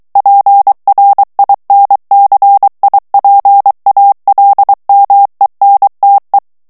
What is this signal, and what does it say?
Morse code, heard as a keyed tone, about 23 wpm, 800 Hz. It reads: PRINCIPALMENTE
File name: morse mp3 / ITALIEN texte clair 1 vitesse 22 mots minute